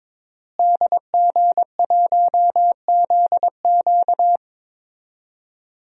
Morse code, heard as a keyed tone, about 22 wpm, 700 Hz. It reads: DG1ZQ
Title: DXCC-100 Call Signs - 5 Characters - Repeated 2x - 22wpm